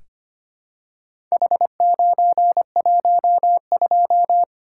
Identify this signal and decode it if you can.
Morse code, heard as a keyed tone, about 25 wpm, 700 Hz. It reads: H912
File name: practice / 25wpm / Four Letter-Number Combinations - 25wpm